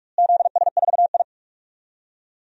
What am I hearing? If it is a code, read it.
Morse code, heard as a keyed tone, about 44 wpm, 690 Hz. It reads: ZS4I
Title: DXCC-100 Call Signs - 4 Characters - Repeated 3x - 45wpm